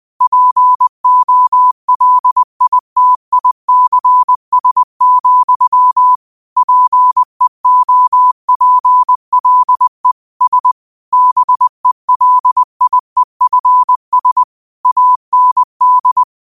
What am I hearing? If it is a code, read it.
Morse code, heard as a keyed tone, about 20 wpm, 1000 Hz. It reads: POLITICS, PEOPLE S BELIEFS AND
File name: qotdfiles / 20wpm